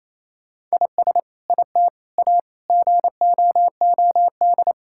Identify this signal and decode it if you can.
Morse code, heard as a keyed tone, about 28 wpm, 700 Hz. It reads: IS IT A GOOD
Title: Sentences from Top 1000 Words - Review Part 5 - 28wpm